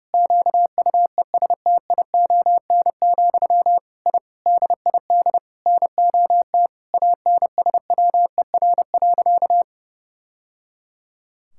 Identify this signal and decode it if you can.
Morse code, heard as a keyed tone, about 30 wpm, 700 Hz. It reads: QUESTION, I DID NOT ANSWER.
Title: Sentences from Top 1000 Words - Review Part 6 - 30wpm